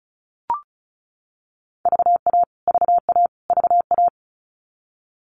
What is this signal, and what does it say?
Morse code, heard as a keyed tone, about 34 wpm, 700 Hz. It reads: VA VA VA
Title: US States Repeated 3x 35wpm